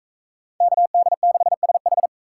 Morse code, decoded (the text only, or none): KD6SH